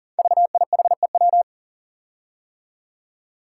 Morse code, heard as a keyed tone, about 38 wpm, 690 Hz. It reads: VIHEW
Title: Code Groups - 5 Letter:Numbers - 40wpm